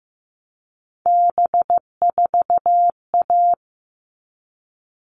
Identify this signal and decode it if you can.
Morse code, heard as a keyed tone, about 15 wpm, 700 Hz. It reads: B4A
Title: All Cut Numbers - 3-Digit 15wpm